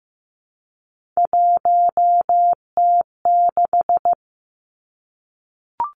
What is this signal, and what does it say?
Morse code, heard as a keyed tone, about 15 wpm, 700 Hz. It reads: 1T6
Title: Three Challenging Letter-Number Combinations 15wpm